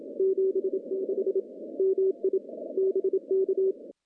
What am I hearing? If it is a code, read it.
Morse code, heard as a keyed tone, about 28 wpm, 380 Hz. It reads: 76 MI BK
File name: contests / CW-SS-k5tr- -1400